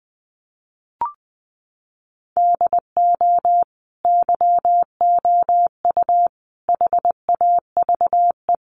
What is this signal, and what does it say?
Morse code, heard as a keyed tone, about 20 wpm, 700 Hz. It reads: DO YOU HAVE
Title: Sentences from Top 1500 Words 20wpm